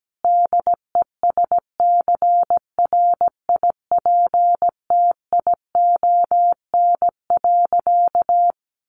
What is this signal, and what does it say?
Morse code, heard as a keyed tone, about 17 wpm, 700 Hz. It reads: DESCRIPTION.
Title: Sentences from Top 1300 Words 17wpm